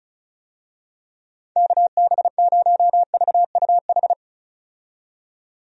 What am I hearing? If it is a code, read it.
Morse code, heard as a keyed tone, about 34 wpm, 700 Hz. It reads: KB0VUH
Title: USA Call Signs - 6 Characters - 35wpm